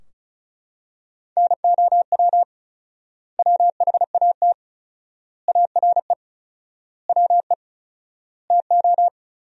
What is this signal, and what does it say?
Morse code, heard as a keyed tone, about 34 wpm, 700 Hz. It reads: NOW WHAT ARE WE TO
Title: 4x Word-Spacing - Sentences from Top 100 Words - 35wpm